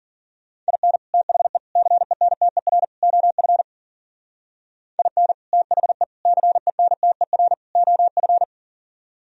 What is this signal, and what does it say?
Morse code, heard as a keyed tone, about 51 wpm, 700 Hz. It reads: IN THE CENTER OF IN THE CENTER OF